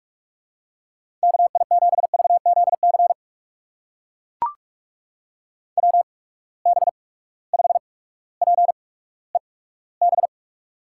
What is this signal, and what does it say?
Morse code, heard as a keyed tone, about 44 wpm, spaced as for 19 wpm, 700 Hz. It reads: KI7VZC WB5PEB
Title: USA Call Signs - 6 Characters - 8x Character Spacing - 45wpm